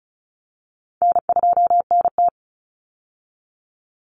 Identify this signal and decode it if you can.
Morse code, heard as a keyed tone, about 34 wpm, 700 Hz. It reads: N2NT
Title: Easy Call Signs Repeated 2x - 35wpm